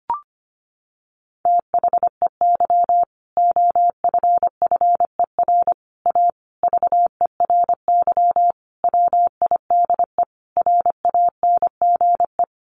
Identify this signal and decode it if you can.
Morse code, heard as a keyed tone, about 25 wpm, 700 Hz. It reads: THEY OFFER A VERY WIDE RANGE
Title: Sentences from Top 900 Words 25wpm